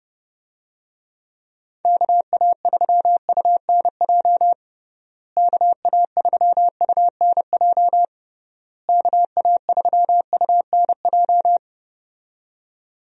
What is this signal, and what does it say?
Morse code, heard as a keyed tone, about 32 wpm, 700 Hz. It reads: KA3UNJ KA3UNJ KA3UNJ
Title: USA Call Signs - 6 Characters - Repeated 3x - 30wpm